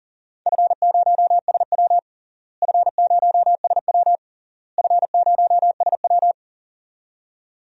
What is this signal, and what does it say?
Morse code, heard as a keyed tone, about 40 wpm, 700 Hz. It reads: F0SW F0SW F0SW
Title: DXCC-100 Call Signs - 4 Characters - Repeated 3x - 40wpm